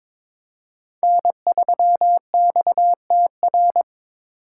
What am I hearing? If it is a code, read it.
Morse code, heard as a keyed tone, about 21 wpm, 700 Hz. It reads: N3XTR
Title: USA Call Signs - 5 Characters - 3x Character Spacing - 22wpm